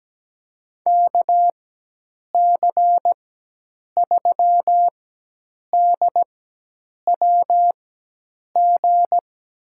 Morse code, heard as a keyed tone, about 17 wpm, spaced as for 8 wpm, 700 Hz. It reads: KC3DWG
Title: USA Call Signs - 4x Character Spacing - 17wpm